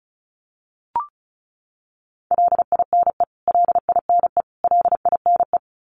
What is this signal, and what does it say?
Morse code, heard as a keyed tone, about 34 wpm, 700 Hz. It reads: LINE LINE LINE
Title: Top 300 Words Repeated 35wpm